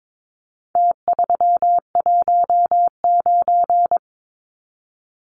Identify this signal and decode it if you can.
Morse code, heard as a keyed tone, about 22 wpm, 700 Hz. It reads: T319
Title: Numbers - Contest Style - 4-Digit 22wpm